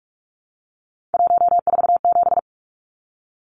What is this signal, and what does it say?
Morse code, heard as a keyed tone, about 44 wpm, 700 Hz. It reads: 147
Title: Numbers - 3-Digit 45wpm